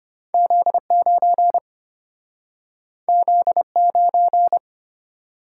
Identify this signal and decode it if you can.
Morse code, heard as a keyed tone, about 33 wpm, 690 Hz. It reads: Z9 Z9